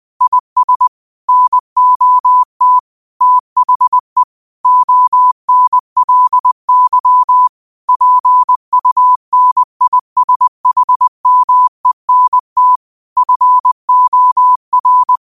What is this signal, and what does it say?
Morse code, heard as a keyed tone, about 20 wpm, 1000 Hz. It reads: IS NOT THE ONLY PUNISHMENT FOR
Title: 20 WPM morse code quotes for Thu, 29 Feb 2024